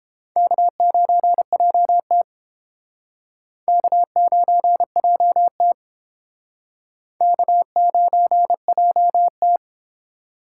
Morse code, heard as a keyed tone, about 31 wpm, 700 Hz. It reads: K9JT K9JT K9JT